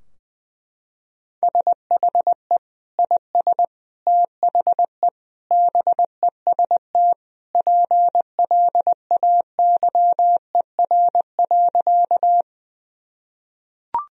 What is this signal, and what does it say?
Morse code, heard as a keyed tone, about 20 wpm, 700 Hz. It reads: SHE IS THE BEST PLAYER.
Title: Sentences from Top 1000 Words - Review Part 4 - 20wpm